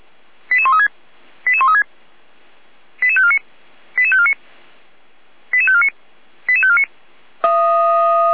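Five-tone selective call (ZVEI), heard as quick runs of five tones, tones of about 70 ms. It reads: 88136 88136 88549 88549 88549 88549
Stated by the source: ZVEI five-tone selective call received on an SDR.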